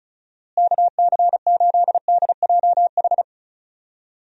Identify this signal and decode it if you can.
Morse code, heard as a keyed tone, about 34 wpm, 700 Hz. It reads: KC8DJH
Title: USA Call Signs - 6 Characters - Repeated 3x - 35wpm